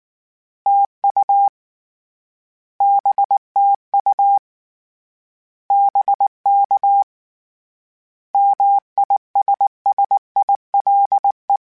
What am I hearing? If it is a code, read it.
Morse code, heard as a keyed tone, about 19 wpm, 800 Hz. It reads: TU BTU BK MISSILE
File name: Misc / Misc-mp3s / Words @18